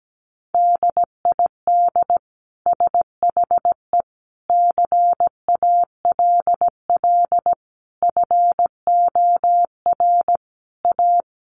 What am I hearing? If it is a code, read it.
Morse code, heard as a keyed tone, about 17 wpm, 700 Hz. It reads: DID SHE CALL FOR A
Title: Sentences from Top 1300 Words 17wpm